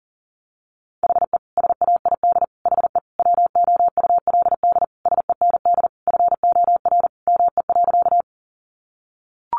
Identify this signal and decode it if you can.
Morse code, heard as a keyed tone, about 40 wpm, 700 Hz. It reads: HE SAID HE WOULD SEND FOR ME.
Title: Sentences from Top 1500 Words 40wpm